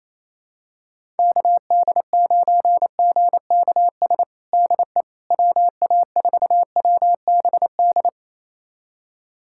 Morse code, heard as a keyed tone, about 27 wpm, 700 Hz. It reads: KD9GKS DE WA4WBD
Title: USA Callsign DE Callsign - 6 Characters - 28wpm